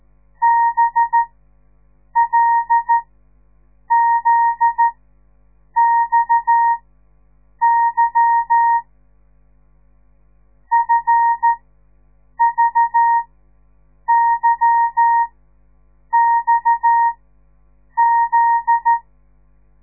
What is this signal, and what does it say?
Morse code, heard as a keyed tone, about 13 wpm, spaced as for 7 wpm, 930 Hz. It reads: BLZXY FVYXZ
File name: radiomorningstar / morse / les 9